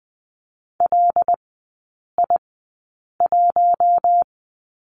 Morse code, heard as a keyed tone, about 20 wpm, spaced as for 8 wpm, 690 Hz. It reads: LI1
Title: 212 Lesson 34e - Number 8 - Three Characters 20wpm